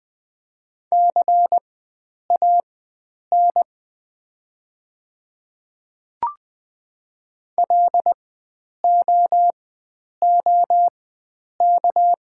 Morse code, heard as a keyed tone, about 20 wpm, spaced as for 9 wpm, 690 Hz. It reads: CAN LOOK
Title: 4x Character Spacing - Top 100 Words - 20wpm